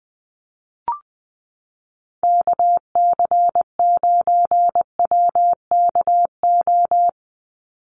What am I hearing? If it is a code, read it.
Morse code, heard as a keyed tone, about 20 wpm, 700 Hz. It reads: KC9WKO
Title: USA Call Signs 20wpm